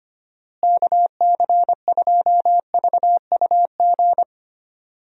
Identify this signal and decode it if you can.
Morse code, heard as a keyed tone, about 25 wpm, 700 Hz. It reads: KC2VUG
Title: USA Call Signs - 6 Characters - Repeated 3x - 25wpm